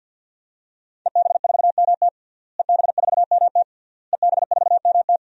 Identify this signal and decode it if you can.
Morse code, heard as a keyed tone, about 50 wpm, 700 Hz. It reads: EB4MT EB4MT EB4MT
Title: DXCC-100 Call Signs - 5 Characters - Repeated 3x - 50wpm